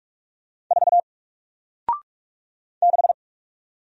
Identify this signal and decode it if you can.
Morse code, heard as a keyed tone, about 46 wpm, 700 Hz. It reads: 4 6
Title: Single Letter-Number 45wpm